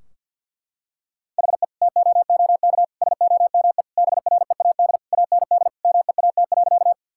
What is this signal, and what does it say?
Morse code, heard as a keyed tone, about 50 wpm, 700 Hz. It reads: HE TOOK SOME BREAD AND MEAT.
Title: Sentences from Top 1600 Words 50wpm